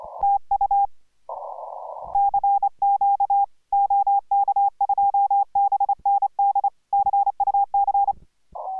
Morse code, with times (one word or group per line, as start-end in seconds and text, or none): 0.2-0.9 TU
2.1-3.4 CQ
3.7-6.7 OK2BND
6.9-8.1 CUC